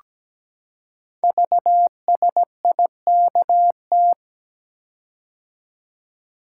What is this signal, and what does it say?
Morse code, heard as a keyed tone, about 17 wpm, 700 Hz. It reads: VSIKT
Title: Code Groups - 5 Letters - 17wpm